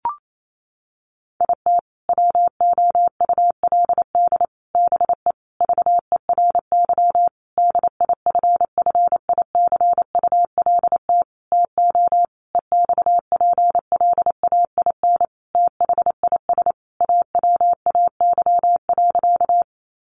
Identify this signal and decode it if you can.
Morse code, heard as a keyed tone, about 28 wpm, 700 Hz. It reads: IT WOULD BE VERY DIFFICULT TO EXPLAIN THIS AWAY.
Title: Sentences from Top 1100 Words 28wpm